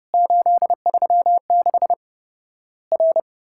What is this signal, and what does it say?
Morse code, heard as a keyed tone, about 30 wpm, 700 Hz. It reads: 836 R
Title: Worked All Europe DX Contest - QTC Exchange - 30wpm